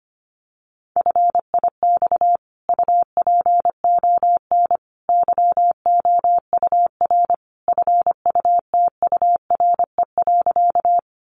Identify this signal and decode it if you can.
Morse code, heard as a keyed tone, about 25 wpm, 700 Hz. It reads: FIX UPON YOUR FUTURE.
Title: Sentences from Top 1400 Words 25wpm